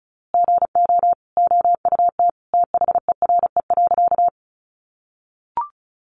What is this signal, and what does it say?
Morse code, heard as a keyed tone, about 34 wpm, 700 Hz. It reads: GO OUT THERE.
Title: A Sound Of Thunder 35wpm